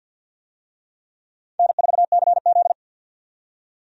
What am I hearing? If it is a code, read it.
Morse code, heard as a keyed tone, about 53 wpm, 700 Hz. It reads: N4CZ
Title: USA Call Signs - 4 Characters - 50wpm